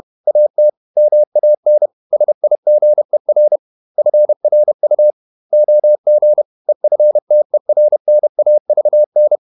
Morse code, heard as a keyed tone, about 31 wpm, 600 Hz. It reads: AT MAN SIGER FRU OG EFTERNAVN
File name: Ferie QRQ 31wpm